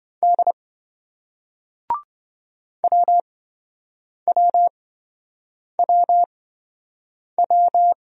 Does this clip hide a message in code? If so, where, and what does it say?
Yes: Morse code from 0.2 to 7.9 s, heard as a keyed tone, about 27 wpm, spaced as for 7 wpm, 700 Hz. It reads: D WWWW